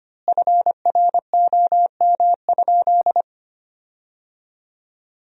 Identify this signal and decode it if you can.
Morse code, heard as a keyed tone, about 25 wpm, 700 Hz. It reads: FROM?
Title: 2x Word-Spacing - Sentences from Top 200 Words - 25wpm